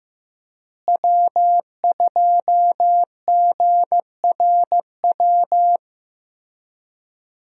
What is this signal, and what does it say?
Morse code, heard as a keyed tone, about 15 wpm, 700 Hz. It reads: W2GRW
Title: USA Call Signs - 5 Characters - Repeated 3x - 15wpm